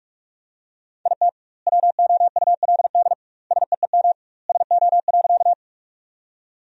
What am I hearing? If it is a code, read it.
Morse code, heard as a keyed tone, about 46 wpm, 700 Hz. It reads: IT WOULD SEEM SO.
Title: 2x Word-Spacing - Sentences from Top 300 Words - 45wpm